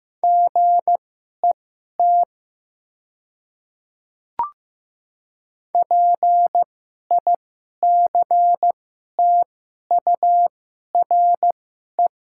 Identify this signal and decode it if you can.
Morse code, heard as a keyed tone, about 15 wpm, spaced as for 11 wpm, 700 Hz. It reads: GET PICTURE
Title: 2x Character Spacing - Top 200 Words - 15wpm